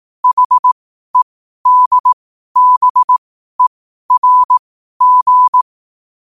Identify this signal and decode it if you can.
Morse code, heard as a keyed tone, about 18 wpm, spaced as for 12 wpm, 1000 Hz. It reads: HEDBERG
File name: qotdfiles / 12wpm